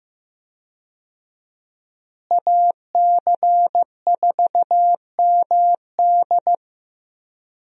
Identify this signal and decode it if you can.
Morse code, heard as a keyed tone, about 16 wpm, 700 Hz. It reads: AC4MD
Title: USA Call Signs - 5 Characters - Repeated 3x - 15wpm